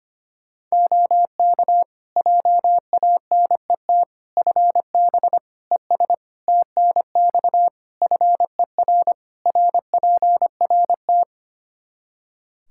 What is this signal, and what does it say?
Morse code, heard as a keyed tone, about 25 wpm, 700 Hz. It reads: OK JANET FB ES TNX FER RPRT
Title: Practice QSO - Exchange 2 - Info - 25wpm